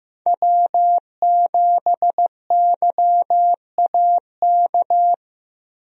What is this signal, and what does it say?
Morse code, heard as a keyed tone, about 15 wpm, 700 Hz. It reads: W7YAK
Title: USA Call Signs - 5 Characters - Repeated 3x - 15wpm